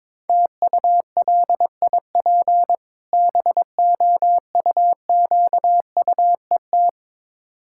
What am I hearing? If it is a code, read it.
Morse code, heard as a keyed tone, about 22 wpm, 690 Hz. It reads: TULIP BOUQUET
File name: Difficult Words with Hints - 22wpm